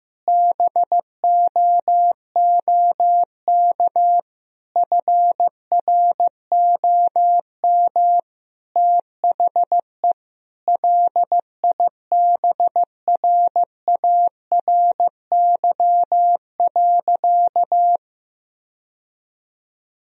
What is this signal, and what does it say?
Morse code, heard as a keyed tone, about 15 wpm, 700 Hz. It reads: BOOK FROM THE LIBRARY.
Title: Sentences from Top 1000 Words - Review Part 4 - 15wpm